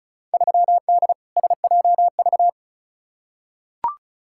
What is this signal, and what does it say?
Morse code, heard as a keyed tone, about 34 wpm, 700 Hz. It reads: 3D SJV
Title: Field Day Exchanges - 35wpm